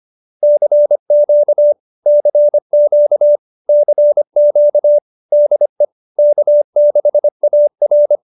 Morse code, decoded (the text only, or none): CQ CQ CQ DE K6AR